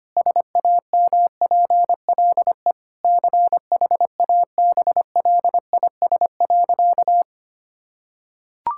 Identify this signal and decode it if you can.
Morse code, heard as a keyed tone, about 25 wpm, 700 Hz. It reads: SAMPLE CHABLIS.
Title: Pangram Challenge v2 - 25wpm